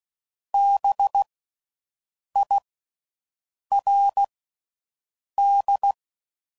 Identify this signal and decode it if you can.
Morse code, heard as a keyed tone, about 16 wpm, 800 Hz. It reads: B I R D
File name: Suffixes / General / Houses